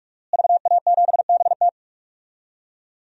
Morse code, heard as a keyed tone, about 46 wpm, 700 Hz. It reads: VA7BT
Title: DXCC-100 Call Signs - 5 Characters - Repeated 2x - 45wpm